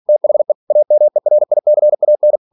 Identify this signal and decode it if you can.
Morse code, heard as a keyed tone, about 46 wpm, 600 Hz. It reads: THE AMERICAN
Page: Never 47wpm